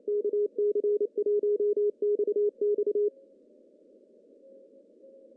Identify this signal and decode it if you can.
Morse code, heard as a keyed tone, about 25 wpm, 380 Hz. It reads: KC1XX
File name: CW / cqww-cw-k5tr